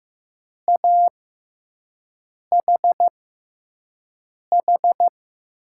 Morse code, heard as a keyed tone, about 15 wpm, spaced as for 5 wpm, 700 Hz. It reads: AHH
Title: USA Call Signs - 5 Characters - 6x Character Spacing - 15wpm